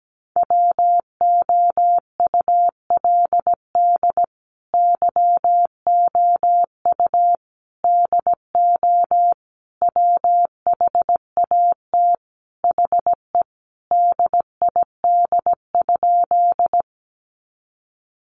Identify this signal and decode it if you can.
Morse code, heard as a keyed tone, about 17 wpm, 700 Hz. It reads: WOULD YOU DO WHAT HE DID?
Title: Sentences from Top 100 Words 17wpm